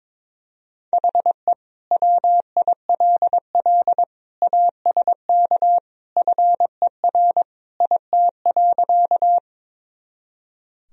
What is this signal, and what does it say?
Morse code, heard as a keyed tone, about 22 wpm, 700 Hz. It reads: HE WILL ASK FER IT.
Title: Sentences from Top 200 Words and Abbreviations 22wpm